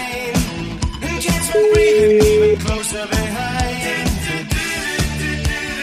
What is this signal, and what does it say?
Ringback tone (Europe, 425 Hz), with noise.